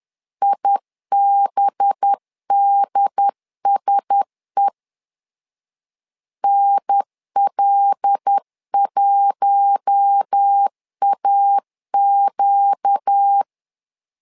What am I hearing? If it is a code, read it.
Morse code, heard as a keyed tone, about 10 wpm, 800 Hz. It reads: IBDSE NL1AQ
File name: LEZIONI / 5 LBVC / 5 LBVC 10wpm 2349